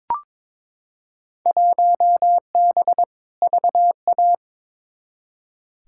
Morse code, heard as a keyed tone, about 21 wpm, 700 Hz. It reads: 1B VA
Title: Field Day Exchanges - 22wpm